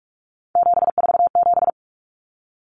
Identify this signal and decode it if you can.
Morse code, heard as a keyed tone, about 44 wpm, 700 Hz. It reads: 747
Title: Common Cut Numbers - 3-Digit 45wpm